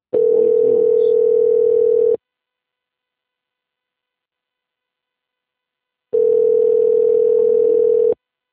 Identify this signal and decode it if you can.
Ringback tone (North America), 440 + 480 Hz.